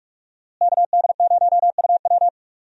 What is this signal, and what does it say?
Morse code, heard as a keyed tone, about 42 wpm, 700 Hz. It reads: KD0UW
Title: USA Call Signs - 5 Characters - Repeated 2x - 45wpm